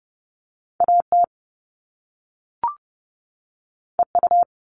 Morse code, heard as a keyed tone, about 30 wpm, 700 Hz. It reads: AT EU
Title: All Cut Numbers - 2-Digit 30wpm